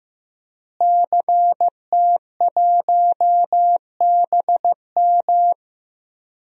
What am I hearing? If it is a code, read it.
Morse code, heard as a keyed tone, about 16 wpm, 700 Hz. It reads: CT1BM
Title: DXCC-100 Call Signs - 5 Characters - Repeated 2x - 15wpm